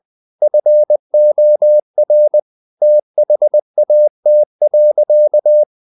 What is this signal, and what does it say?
Morse code, heard as a keyed tone, about 20 wpm, 600 Hz. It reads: FOR THAT.
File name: the old man and the sea part 2 20wpm